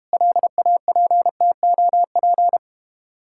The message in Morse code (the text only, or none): LAPTOP